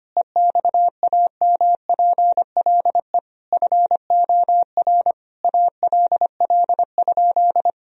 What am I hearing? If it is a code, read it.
Morse code, heard as a keyed tone, about 25 wpm, 700 Hz. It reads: EXAMPLE FOR ALL?
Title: Sentences from Top 1000 Words - Review Part 3 - 25wpm